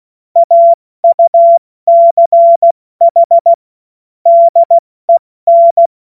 Morse code, heard as a keyed tone, about 16 wpm, spaced as for 14 wpm, 680 Hz. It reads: AUCH DEN
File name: Morsekurs / Witze / Witz 03 Juni Mai 2025 70 F80